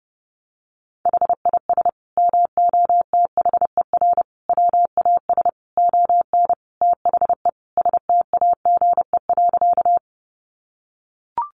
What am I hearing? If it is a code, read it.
Morse code, heard as a keyed tone, about 30 wpm, 700 Hz. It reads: HIS MOTHER WAS ON THE STAGE.
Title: Sentences from Top 1200 Words 30wpm